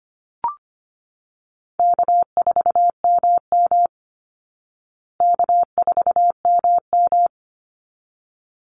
Morse code, heard as a keyed tone, about 25 wpm, 700 Hz. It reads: K4MM K4MM
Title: Call Signs Repeated 3x - 25wpm